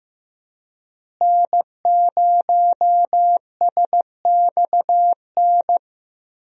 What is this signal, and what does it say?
Morse code, heard as a keyed tone, about 15 wpm, 700 Hz. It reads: N0SXN
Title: USA Call Signs - 5 Characters - Repeated 2x - 15wpm